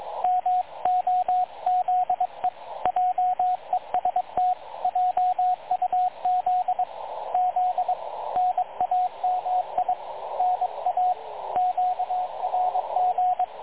Morse code, heard as a keyed tone, about 22 wpm, 730 Hz, with radio noise. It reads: MOZE JEST JUZ Z NAMI NA QRG
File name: dlazaawansowanych / nagraniazpasma / websdr recording 3567.0kHz